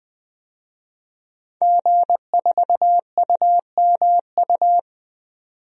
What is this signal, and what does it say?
Morse code, heard as a keyed tone, about 21 wpm, 700 Hz. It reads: G4UMU
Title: DXCC-100 Call Signs - 5 Characters - 20wpm